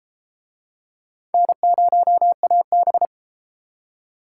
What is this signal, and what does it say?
Morse code, heard as a keyed tone, about 33 wpm, 700 Hz. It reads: N0AB